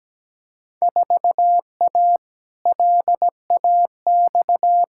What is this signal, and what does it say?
Morse code, heard as a keyed tone, about 17 wpm, 700 Hz. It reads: 4A LAX
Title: Field Day Exchanges - 17wpm